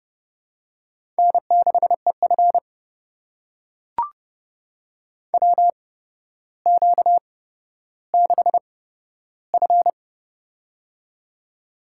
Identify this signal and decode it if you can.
Morse code, heard as a keyed tone, about 30 wpm, spaced as for 13 wpm, 700 Hz. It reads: N6EF WQ6F
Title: USA Call Signs - 4 Characters - 8x Character Spacing - 30wpm